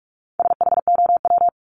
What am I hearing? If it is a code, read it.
Morse code, heard as a keyed tone, about 44 wpm, 700 Hz. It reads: SHOW
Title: ES:S Suffix - Top 100 Words with Base Words 45wpm